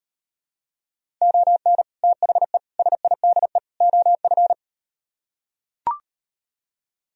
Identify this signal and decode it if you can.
Morse code, heard as a keyed tone, about 38 wpm, 700 Hz. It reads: ON THE SIDE OF